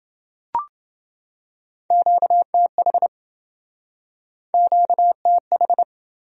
Morse code, heard as a keyed tone, about 27 wpm, 700 Hz. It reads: QTH QTH